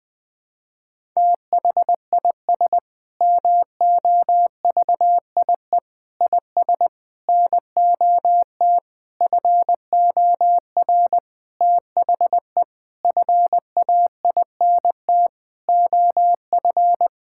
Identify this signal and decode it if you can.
Morse code, heard as a keyed tone, about 21 wpm, 700 Hz. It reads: THIS MOVIE IS NOT FOR THE FAINT OF
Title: Sentences with 7-Word Phrases - 20wpm